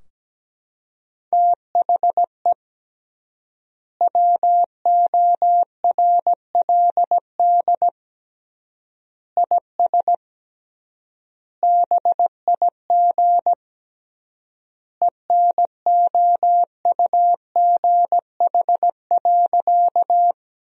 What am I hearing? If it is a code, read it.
Morse code, heard as a keyed tone, about 17 wpm, 700 Hz. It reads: THE WORLD IS BIG ENOUGH.
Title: 3x Word-Spacing - Sentences from Top 400 Words - 17wpm